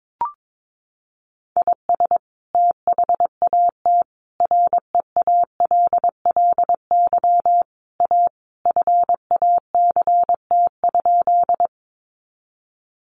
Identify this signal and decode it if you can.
Morse code, heard as a keyed tone, about 22 wpm, 700 Hz. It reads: IS THAT REALLY A FACT?
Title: Sentences from Top 600 Words 22wpm